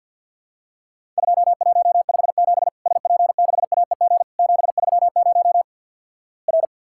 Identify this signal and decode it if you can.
Morse code, heard as a keyed tone, about 50 wpm, 700 Hz. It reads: 2157 SP6AEG 730 R
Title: Worked All Europe DX Contest - QTC Exchange - 50wpm